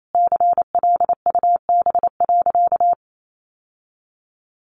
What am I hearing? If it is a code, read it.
Morse code, heard as a keyed tone, about 28 wpm, 700 Hz. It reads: CLUB.
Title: Sentences from Top 1200 Words 28wpm